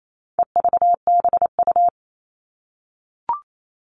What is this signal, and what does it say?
Morse code, heard as a keyed tone, about 28 wpm, 700 Hz. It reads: EVBU
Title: All Cut Numbers - 4-Digit 28wpm